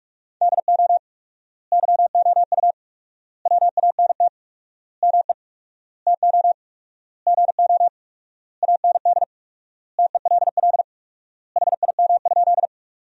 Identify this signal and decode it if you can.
Morse code, heard as a keyed tone, about 46 wpm, 700 Hz. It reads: DO YOU WANT ME TO GO AND TELL HIM?
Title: 4x Word-Spacing - Sentences from Top 200 Words - 45wpm